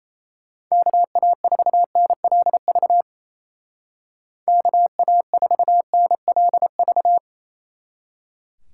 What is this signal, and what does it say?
Morse code, heard as a keyed tone, about 33 wpm, 700 Hz. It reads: KA4NLV KA4NLV
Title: Speed Racing - USA 2x3 Call Signs 28wpm